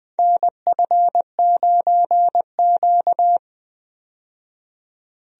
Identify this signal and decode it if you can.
Morse code, heard as a keyed tone, about 20 wpm, 700 Hz. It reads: NF9Q
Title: USA Call Signs - 4 Characters - 20wpm